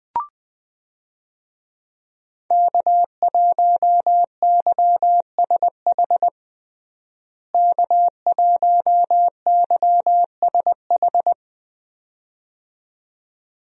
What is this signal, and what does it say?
Morse code, heard as a keyed tone, about 20 wpm, 700 Hz. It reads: K1YSH K1YSH
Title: USA Call Signs - 5 Characters - Repeated 2x - 20wpm